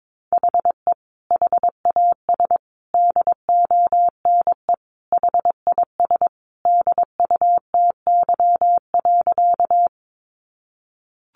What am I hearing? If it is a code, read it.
Morse code, heard as a keyed tone, about 22 wpm, 700 Hz. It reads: HE HAS DONE HIS DUTY.
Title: Sentences from Top 1500 Words 22wpm